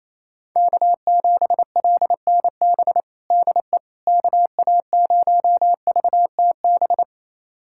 Morse code, heard as a keyed tone, about 28 wpm, 700 Hz. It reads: K7LNB DE KA0VTB
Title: USA Callsign DE Callsign 28wpm